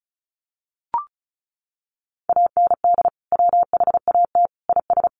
Morse code, heard as a keyed tone, about 34 wpm, 700 Hz. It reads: AND WHAT IS
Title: Sentences from Top 700 Words 35wpm